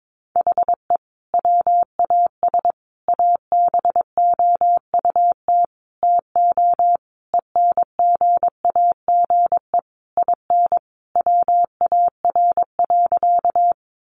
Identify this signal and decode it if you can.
Morse code, heard as a keyed tone, about 22 wpm, 700 Hz. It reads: HE WAS ABOUT TO ENGAGE IN WAR.
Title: Sentences from Top 1500 Words 22wpm